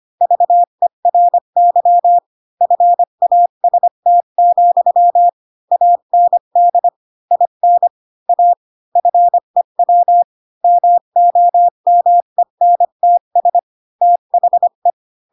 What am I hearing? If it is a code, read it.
Morse code, heard as a keyed tone, about 25 wpm, 700 Hz. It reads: VERY FAST, AND IN A FEW MOMENTS THE
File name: Pinocchio / Pinocchio ch5 20 25 ik1hge